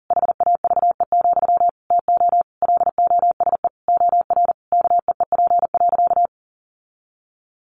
Morse code, heard as a keyed tone, about 40 wpm, 700 Hz. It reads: HAVE, TO LOSE OR KEEP.
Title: Sentences from Top 1100 Words 40wpm